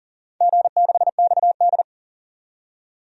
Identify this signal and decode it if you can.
Morse code, heard as a keyed tone, about 40 wpm, 700 Hz. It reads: G6XD
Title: DXCC-100 Call Signs - 4 Characters - 40wpm